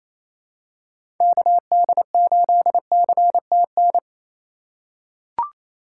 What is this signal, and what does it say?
Morse code, heard as a keyed tone, about 28 wpm, 700 Hz. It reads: KD8CTN
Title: USA Call Signs - 3x Character Spacing - 28wpm